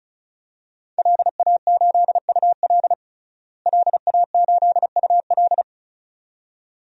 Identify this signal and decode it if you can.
Morse code, heard as a keyed tone, about 36 wpm, 700 Hz. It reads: LA8UL LA8UL
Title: DXCC-100 Call Signs - 5 Characters - Repeated 2x - 35wpm